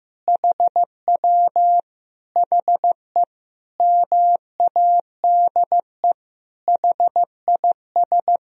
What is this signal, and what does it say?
Morse code, heard as a keyed tone, about 15 wpm, 700 Hz. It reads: HW HE MADE HIS
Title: Sentences from Top 300 Words and Abbreviations 15wpm